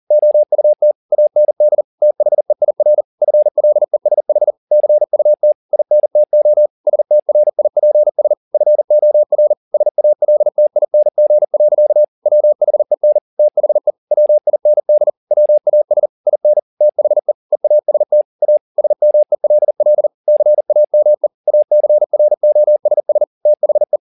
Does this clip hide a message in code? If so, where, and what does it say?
Yes: Morse code from 0.1 to 24.0 s, heard as a keyed tone, about 40 wpm, 600 Hz. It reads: OUT AND THEIR FLESH CUT INTO STRIPS FOR SALTING. WHEN THE WIND WAS IN THE EAST A SMELL CAME ACROSS THE